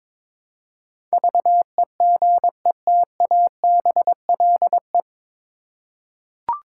Morse code, heard as a keyed tone, about 22 wpm, 700 Hz. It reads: VEGETABLE
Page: Morse Code Ninja practice 22wpm